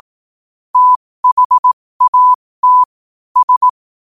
Morse code, heard as a keyed tone, about 18 wpm, spaced as for 15 wpm, 1000 Hz. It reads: THAT S
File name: qotdfiles / 15wpm